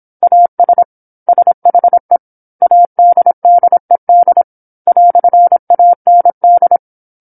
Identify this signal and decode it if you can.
Morse code, heard as a keyed tone, about 27 wpm, 700 Hz. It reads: AS SHE ADDED "AND
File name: 08-Chapter-08 26 WPM